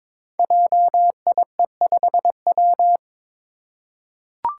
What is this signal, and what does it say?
Morse code, heard as a keyed tone, about 22 wpm, 690 Hz. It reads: JIE5W
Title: Code Groups - 5 Letter:Numbers - 22wpm